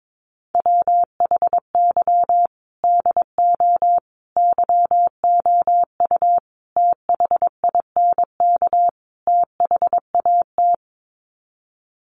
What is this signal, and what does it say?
Morse code, heard as a keyed tone, about 22 wpm, 700 Hz. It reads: WHY DO YOU THINK THAT
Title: Sets of 5 Words Encore2 22wpm